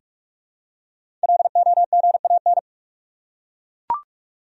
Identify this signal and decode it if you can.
Morse code, heard as a keyed tone, about 42 wpm, 700 Hz. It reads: LOGAN
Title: USA Common Names - 45wpm